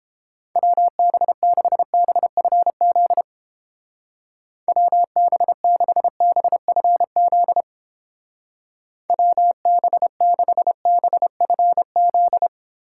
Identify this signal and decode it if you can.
Morse code, heard as a keyed tone, about 29 wpm, 700 Hz. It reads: WB6BFZ WB6BFZ WB6BFZ